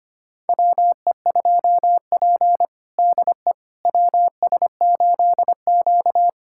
Morse code, heard as a keyed tone, about 25 wpm, 700 Hz. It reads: WE2P DE WS8Q
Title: USA Callsign DE Callsign - 4 Characters - 25wpm